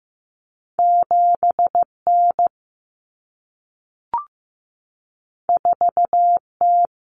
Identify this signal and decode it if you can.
Morse code, heard as a keyed tone, about 15 wpm, 700 Hz. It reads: 7N 4T
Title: Common Cut Numbers - 2-Digit 15wpm